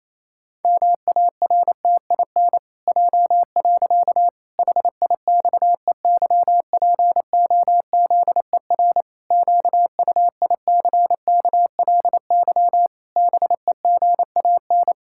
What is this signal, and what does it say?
Morse code, heard as a keyed tone, about 28 wpm, 700 Hz. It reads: MARTIN J. HIXEYPOZER QUICKLY BEGAN
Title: Pangram Challenge v2 - 28wpm